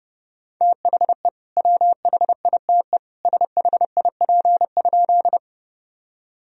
Morse code, heard as a keyed tone, about 30 wpm, 700 Hz. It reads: THE WHITE SHIP?
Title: Sentences from Top 1000 Words - Review Part 2 - 30wpm